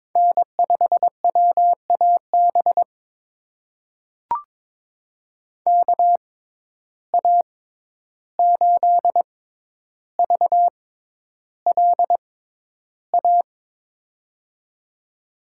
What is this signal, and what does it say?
Morse code, heard as a keyed tone, about 22 wpm, spaced as for 11 wpm, 700 Hz. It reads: N5WAB KA8VLA
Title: USA Call Signs - 6x Character Spacing - 22wpm